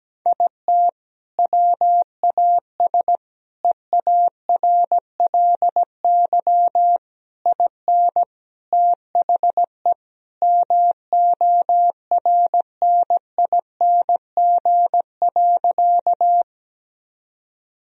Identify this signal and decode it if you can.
Morse code, heard as a keyed tone, about 17 wpm, 700 Hz. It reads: IT WAS EARLY IN THE MORNING.
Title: Sentences from Top 1000 Words - Review Part 2 - 17wpm